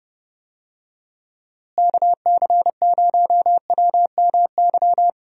Morse code, heard as a keyed tone, about 30 wpm, 700 Hz. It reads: KC0WMY
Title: USA Call Signs - 6 Characters - Repeated 2x - 30wpm